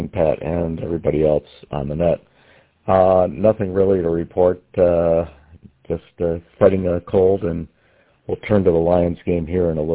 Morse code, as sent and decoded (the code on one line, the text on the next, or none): none